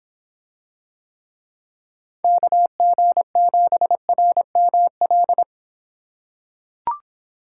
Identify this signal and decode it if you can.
Morse code, heard as a keyed tone, about 26 wpm, 700 Hz. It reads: KG7RML